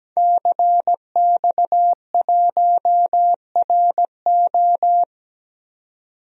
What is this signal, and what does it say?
Morse code, heard as a keyed tone, about 17 wpm, 700 Hz. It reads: CX1RO
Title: DXCC-100 Call Signs - 5 Characters - 17wpm